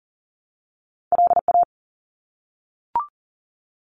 Morse code, heard as a keyed tone, about 39 wpm, 700 Hz. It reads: LA
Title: US States 40wpm